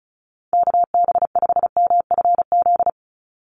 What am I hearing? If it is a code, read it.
Morse code, heard as a keyed tone, about 34 wpm, 700 Hz. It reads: KB5MFZ
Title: USA Call Signs 35wpm